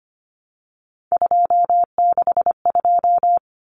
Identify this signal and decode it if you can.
Morse code, heard as a keyed tone, about 25 wpm, 700 Hz. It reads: 262
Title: Numbers - 3-Digit 25wpm